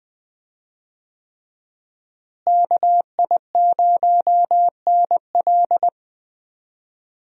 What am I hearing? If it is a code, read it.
Morse code, heard as a keyed tone, about 21 wpm, 700 Hz. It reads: KI0NL
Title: USA Call Signs - 5 Characters - Repeated 3x - 20wpm